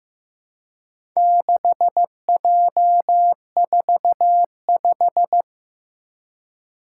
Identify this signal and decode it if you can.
Morse code, heard as a keyed tone, about 15 wpm, 700 Hz. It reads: 6J45
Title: Four Challenging Letter-Number Combinations 15wpm